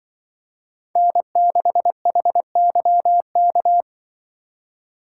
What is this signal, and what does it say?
Morse code, heard as a keyed tone, about 23 wpm, 700 Hz. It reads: N6HYK
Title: Speed Racing - USA 1x3 Call Signs 20wpm